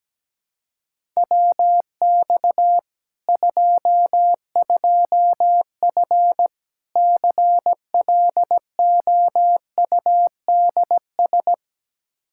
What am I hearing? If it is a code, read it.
Morse code, heard as a keyed tone, about 17 wpm, 700 Hz. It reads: WX 22F CLOUDS
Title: Practice QSO - Exchange 2 - Info - 17wpm